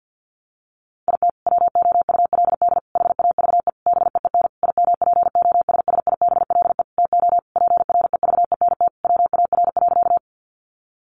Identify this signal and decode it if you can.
Morse code, heard as a keyed tone, about 50 wpm, 700 Hz. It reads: IT WOULD HAVE BEEN IMPOSSIBLE TO PREVENT WAR.
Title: Sentences from Top 1100 Words 50wpm